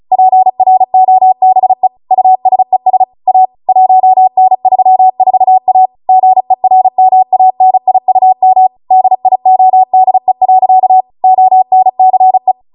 Morse code, heard as a keyed tone, about 36 wpm, 750 Hz. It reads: PROBE USES A 1N34A GERMANIUM DIODE. ONCE